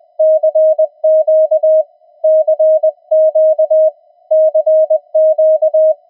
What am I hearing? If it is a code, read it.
Morse code, heard as a keyed tone, about 20 wpm, 630 Hz. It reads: CQ CQ CQ
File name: attachments / TX 7030